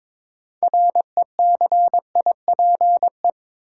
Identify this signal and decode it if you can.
Morse code, heard as a keyed tone, about 22 wpm, 700 Hz. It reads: RECIPE
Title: Sentences from Top 1200 Words - New Word Drill - 22wpm